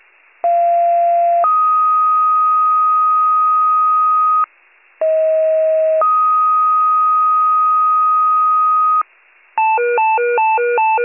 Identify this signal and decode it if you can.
Two-tone page, heard as a one-second tone, then a three-second tone, about 690 then 1250 Hz, about 640 then 1250 Hz. It is heard from 0.4 to 9.0 s.